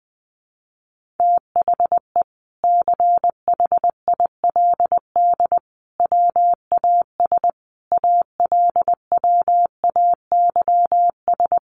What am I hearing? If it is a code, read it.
Morse code, heard as a keyed tone, about 20 wpm, 700 Hz. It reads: THE CHILD WAS ALWAYS
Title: Sentences from Top 1400 Words 20wpm